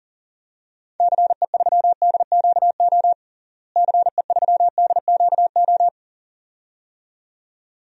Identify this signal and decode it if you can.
Morse code, heard as a keyed tone, about 37 wpm, 700 Hz. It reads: CE3DQO CE3DQO
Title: DXCC-100 Call Signs - 6 Characters - Repeated 2x - 40wpm